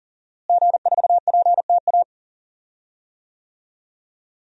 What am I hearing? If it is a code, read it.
Morse code, heard as a keyed tone, about 40 wpm, 700 Hz. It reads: G4PTA
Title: DXCC-100 Call Signs - 5 Characters - Repeated 3x - 40wpm